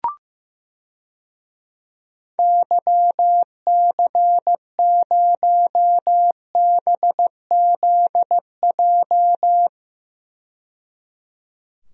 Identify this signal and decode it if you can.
Morse code, heard as a keyed tone, about 15 wpm, 700 Hz. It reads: YC0BZJ
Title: DXCC-100 Call Signs - 6 Characters - 15wpm